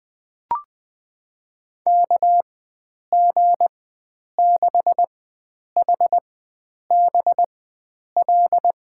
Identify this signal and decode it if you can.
Morse code, heard as a keyed tone, about 20 wpm, spaced as for 9 wpm, 700 Hz. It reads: KG6HBL
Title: USA Call Signs - 6 Characters - 4x Character Spacing - 20wpm